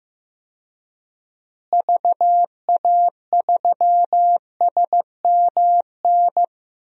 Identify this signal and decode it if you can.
Morse code, heard as a keyed tone, about 16 wpm, 700 Hz. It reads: VA3SMN
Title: DXCC-100 Call Signs - 6 Characters - Repeated 2x - 15wpm